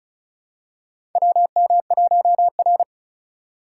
Morse code, heard as a keyed tone, about 36 wpm, 700 Hz. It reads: WM1R
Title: USA Call Signs - 4 Characters - 2x Character Spacing - 35wpm